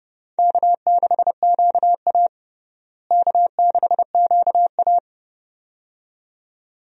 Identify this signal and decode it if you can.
Morse code, heard as a keyed tone, about 30 wpm, 700 Hz. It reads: K6QA K6QA
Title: USA Call Signs - 4 Characters - Repeated 2x - 30wpm